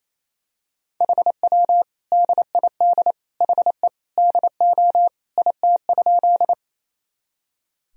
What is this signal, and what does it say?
Morse code, heard as a keyed tone, about 28 wpm, 700 Hz. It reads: HW DID HE DO IT?
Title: Sentences from Top 100 Words and Abbreviations 28wpm